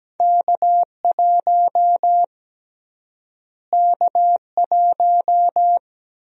Morse code, heard as a keyed tone, about 17 wpm, 700 Hz. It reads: K1 K1
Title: USA Call Sign Prefixes - Repeated 2x - 17wpm